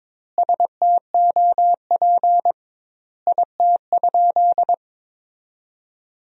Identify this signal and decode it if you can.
Morse code, heard as a keyed tone, about 21 wpm, 700 Hz. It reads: STOP IT?
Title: 2x Word-Spacing - Sentences from Top 300 Words - 22wpm